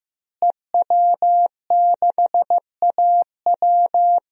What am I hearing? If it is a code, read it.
Morse code, heard as a keyed tone, about 15 wpm, 700 Hz. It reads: EW6AW
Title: DXCC-100 Call Signs - 5 Characters - Repeated 2x - 15wpm